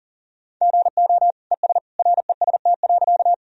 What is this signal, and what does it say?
Morse code, heard as a keyed tone, about 37 wpm, 700 Hz. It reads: GO ES REST.
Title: Sentences from Top 500 Words and Abbreviations 40wpm